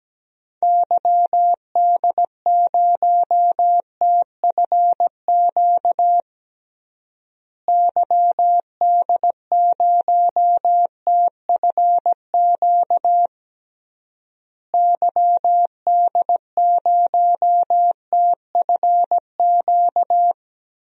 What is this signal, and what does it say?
Morse code, heard as a keyed tone, about 17 wpm, 700 Hz. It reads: YD0TFQ YD0TFQ YD0TFQ
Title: DXCC-100 Call Signs - 6 Characters - Repeated 3x - 17wpm